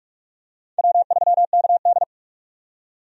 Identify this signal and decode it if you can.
Morse code, heard as a keyed tone, about 41 wpm, 700 Hz. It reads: W3KD
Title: USA Call Signs - 4 Characters - 45wpm